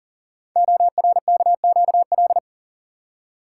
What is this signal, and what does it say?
Morse code, heard as a keyed tone, about 42 wpm, 690 Hz. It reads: ORKQL
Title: Code Groups - 5 Letters - 40wpm